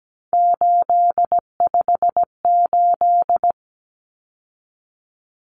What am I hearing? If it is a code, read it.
Morse code, heard as a keyed tone, about 17 wpm, 700 Hz. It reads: 858
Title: Numbers - 3-Digit 17wpm